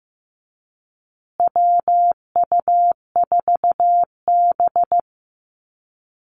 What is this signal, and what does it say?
Morse code, heard as a keyed tone, about 15 wpm, 700 Hz. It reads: WU4B
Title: Call Signs Repeated 2x - 15wpm